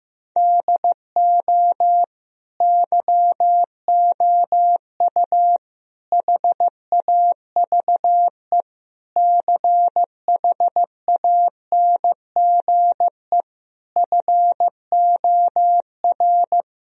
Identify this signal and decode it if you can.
Morse code, heard as a keyed tone, about 15 wpm, 700 Hz. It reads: DO YOU HAVE CHANGE FOR
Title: Sentences from Top 1000 Words - Review Part 2 - 15wpm